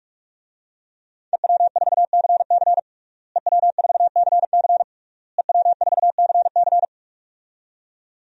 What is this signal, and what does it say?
Morse code, heard as a keyed tone, about 46 wpm, 700 Hz. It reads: EW4CC EW4CC EW4CC
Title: DXCC-100 Call Signs - 5 Characters - Repeated 3x - 45wpm